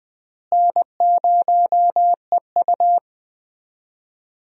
Morse code, heard as a keyed tone, about 20 wpm, 700 Hz. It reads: N0EU
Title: USA Call Signs - 4 Characters - Repeated 3x - 20wpm